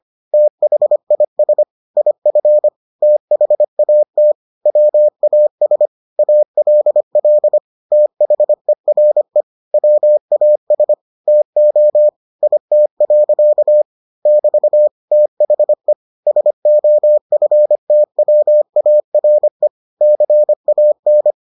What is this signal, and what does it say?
Morse code, heard as a keyed tone, about 25 wpm, 600 Hz. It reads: THIS IF THAT WAS ALL THERE WAS TO IT. <BT> THE SOFTWARE CAN